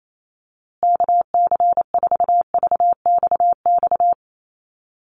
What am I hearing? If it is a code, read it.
Morse code, heard as a keyed tone, about 28 wpm, 700 Hz. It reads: KC4VXX
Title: USA Call Signs Repeated 2x 28wpm